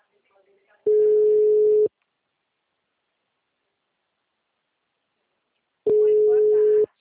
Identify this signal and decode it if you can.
Ringback tone (Europe, 425 Hz), with speech.